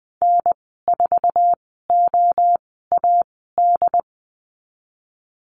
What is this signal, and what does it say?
Morse code, heard as a keyed tone, about 20 wpm, spaced as for 14 wpm, 700 Hz. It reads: N4OAD
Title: 076 Lesson 13g - Letter L - Call Signs 20wpm